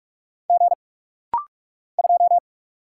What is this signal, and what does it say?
Morse code, heard as a keyed tone, about 45 wpm, 700 Hz. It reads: G 2
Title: Single Letter-Number - Rapid-Fire 45wpm